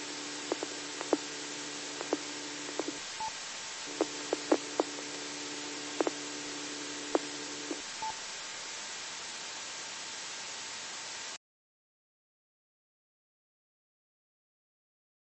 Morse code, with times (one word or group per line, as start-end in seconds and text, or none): none